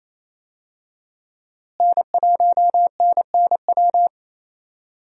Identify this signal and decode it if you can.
Morse code, heard as a keyed tone, about 28 wpm, 700 Hz. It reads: N1NNW
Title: USA Call Signs - 5 Characters - Repeated 3x - 28wpm